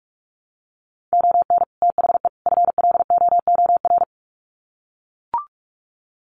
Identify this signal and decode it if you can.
Morse code, heard as a keyed tone, about 44 wpm, 700 Hz. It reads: ON THE FLOOR
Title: Sets of 3 Words 45wpm